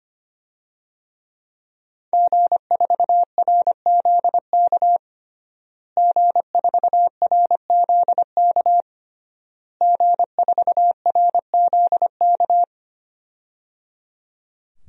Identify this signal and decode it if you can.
Morse code, heard as a keyed tone, about 25 wpm, 700 Hz. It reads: G4RZK G4RZK G4RZK
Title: DXCC-100 Call Signs - 5 Characters - Repeated 3x - 25wpm